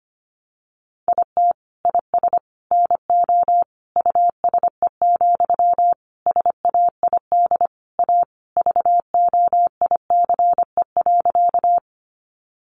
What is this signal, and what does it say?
Morse code, heard as a keyed tone, about 25 wpm, 700 Hz. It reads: IT IS NO USE, SAID A VOICE.
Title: Sentences from Top 500 Words 25wpm